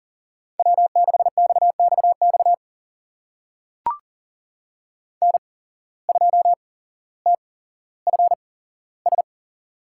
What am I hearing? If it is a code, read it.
Morse code, heard as a keyed tone, about 40 wpm, spaced as for 17 wpm, 700 Hz. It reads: W6XXX N2TFS
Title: USA Call Signs - 5 Characters - 8x Character Spacing - 40wpm